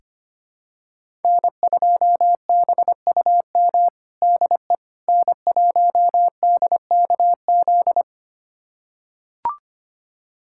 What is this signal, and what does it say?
Morse code, heard as a keyed tone, about 25 wpm, 700 Hz. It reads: N2BUM DE N1DKZ
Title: USA Callsign DE Callsign - 5 Characters - 25wpm